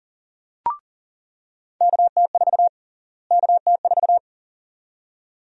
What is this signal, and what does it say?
Morse code, heard as a keyed tone, about 40 wpm, 700 Hz. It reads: KT4 KT4
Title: USA Call Sign Prefixes - Repeated 2x - 40wpm